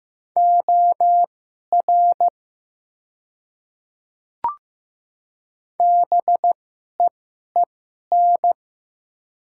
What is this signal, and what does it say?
Morse code, heard as a keyed tone, about 15 wpm, spaced as for 11 wpm, 700 Hz. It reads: OR BEEN
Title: 2x Character Spacing - Top 100 Words - 15wpm